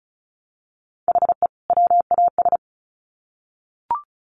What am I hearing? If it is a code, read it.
Morse code, heard as a keyed tone, about 34 wpm, 700 Hz. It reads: HE WAS
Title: Sets of 2 Words 35wpm